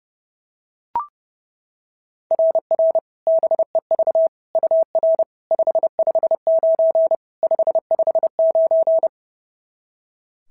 Morse code, heard as a keyed tone, about 30 wpm, 690 Hz. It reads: RR BEV UR 559 559
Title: SOTA QSO Practice 30wpm